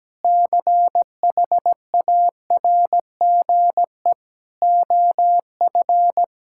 Morse code, heard as a keyed tone, about 17 wpm, 700 Hz. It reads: CHARGE OF
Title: Sentences from Top 1000 Words - Review Part 6 - 17wpm